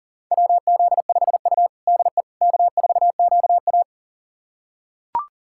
Morse code, heard as a keyed tone, about 40 wpm, 700 Hz. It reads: WZ5U DE K4QA
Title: USA Callsign DE Callsign - 4 Characters - 40wpm